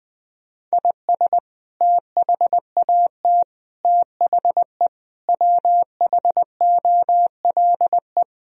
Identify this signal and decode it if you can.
Morse code, heard as a keyed tone, about 19 wpm, 700 Hz. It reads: IS THAT THE WHOLE
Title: Sentences from Top 1000 Words - Review Part 4 - 20wpm